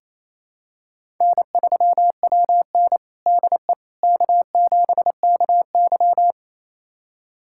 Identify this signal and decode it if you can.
Morse code, heard as a keyed tone, about 28 wpm, 700 Hz. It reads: N3WN DE K7KY
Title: USA Callsign DE Callsign - 4 Characters - 28wpm